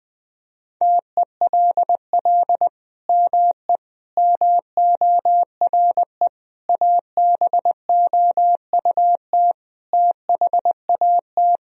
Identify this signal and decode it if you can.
Morse code, heard as a keyed tone, about 21 wpm, 700 Hz. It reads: TELL ME MORE ABOUT THAT
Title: Sentences from Top 1000 Words - Review Part 3 - 20wpm